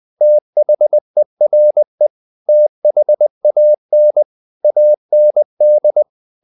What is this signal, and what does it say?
Morse code, heard as a keyed tone, about 20 wpm, 600 Hz. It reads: THERE THAN AND
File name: CW / Practice / 500MostCommonEnglishWords20